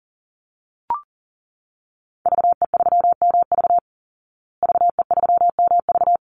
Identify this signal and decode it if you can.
Morse code, heard as a keyed tone, about 40 wpm, 700 Hz. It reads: VE3MV VE3MV
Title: Call Signs Repeated 3x - 40wpm